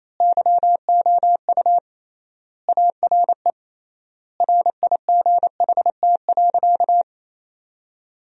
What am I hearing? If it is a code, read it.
Morse code, heard as a keyed tone, about 28 wpm, 700 Hz. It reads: YOU ARE RIGHT.
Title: 3x Word-Spacing - Sentences from Top 500 Words - 28wpm